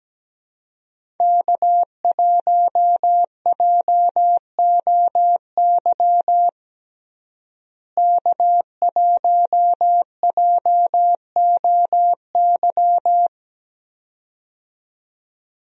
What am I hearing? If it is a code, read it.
Morse code, heard as a keyed tone, about 17 wpm, 700 Hz. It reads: K1JOY K1JOY
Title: USA Call Signs - 5 Characters - Repeated 2x - 17wpm